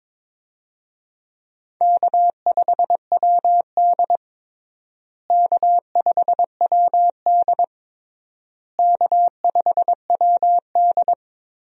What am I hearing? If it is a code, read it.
Morse code, heard as a keyed tone, about 22 wpm, 700 Hz. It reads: K5WD K5WD K5WD
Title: USA Call Signs - 4 Characters - Repeated 3x - 22wpm